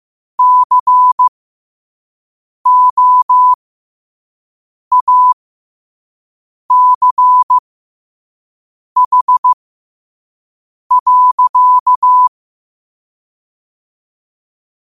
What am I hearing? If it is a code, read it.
Morse code, heard as a keyed tone, about 15 wpm, spaced as for 5 wpm, 1000 Hz. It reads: COACH.
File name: qotdfiles / 5wpm